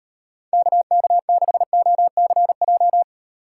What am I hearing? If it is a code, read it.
Morse code, heard as a keyed tone, about 38 wpm, 700 Hz. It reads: KK6OCJ